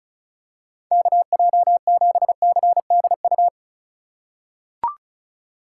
Morse code, heard as a keyed tone, about 36 wpm, 700 Hz. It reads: KJ7CDU
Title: USA Call Signs - 4x Character Spacing - 35wpm